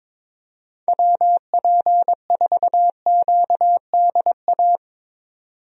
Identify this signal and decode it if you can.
Morse code, heard as a keyed tone, about 22 wpm, 700 Hz. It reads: WP4QDA
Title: USA Call Signs - 6 Characters - 22wpm